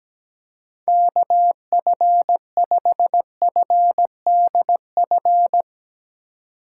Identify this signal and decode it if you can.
Morse code, heard as a keyed tone, about 17 wpm, 700 Hz. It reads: KF5FDF
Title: USA Call Signs - 6 Characters - 2x Character Spacing - 17wpm